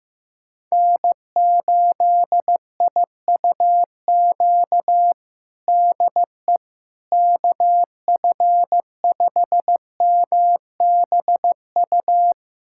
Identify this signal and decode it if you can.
Morse code, heard as a keyed tone, about 15 wpm, 700 Hz. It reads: N8IUQ DE KF5MBU
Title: USA Callsign DE Callsign 15wpm